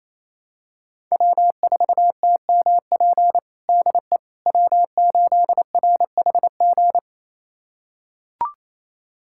Morse code, heard as a keyed tone, about 27 wpm, 700 Hz. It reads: W4TMP DE W8RHG
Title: USA Callsign DE Callsign - 5 Characters - 28wpm